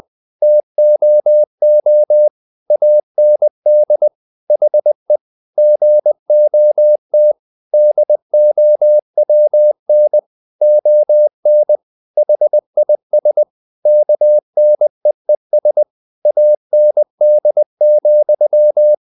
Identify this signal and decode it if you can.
Morse code, heard as a keyed tone, about 20 wpm, 600 Hz. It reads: TOO AND HE GOT DOWN ON HIS KNEES AND,
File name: the old man and the sea / the old man and the sea part 1 20wpm